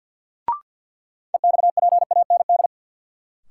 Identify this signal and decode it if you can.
Morse code, heard as a keyed tone, about 52 wpm, 700 Hz. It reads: EXPAND
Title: Sentences from Top 1400 Words - New Word Drill - 50wpm